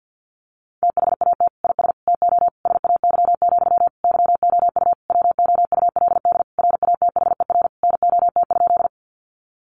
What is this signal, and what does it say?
Morse code, heard as a keyed tone, about 50 wpm, 700 Hz. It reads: THAT IS TO SAY, YOU WOULD RATHER NOT?
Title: Sentences from Top 700 Words 50wpm